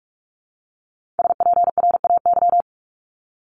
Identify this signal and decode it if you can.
Morse code, heard as a keyed tone, about 46 wpm, 700 Hz. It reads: SPRAY
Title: Multi-letter Phonemes Repeated 3x 45wpm